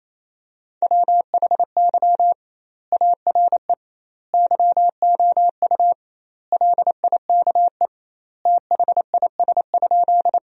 Morse code, heard as a keyed tone, about 28 wpm, 700 Hz. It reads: WHY ARE YOU LIKE THIS?
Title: 2x Word-Spacing - Sentences from Top 200 Words - 28wpm